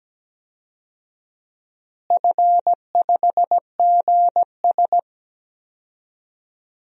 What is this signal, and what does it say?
Morse code, heard as a keyed tone, about 17 wpm, 700 Hz. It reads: F5GS
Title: DXCC-100 Call Signs - 4 Characters - Repeated 2x - 17wpm